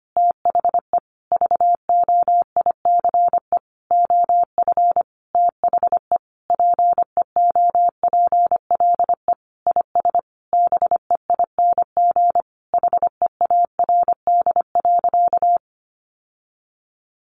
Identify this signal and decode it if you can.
Morse code, heard as a keyed tone, about 25 wpm, 700 Hz. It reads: THE VOICE OF THE PEOPLE IS BEING HEARD.
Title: Sentences from Top 1500 Words 25wpm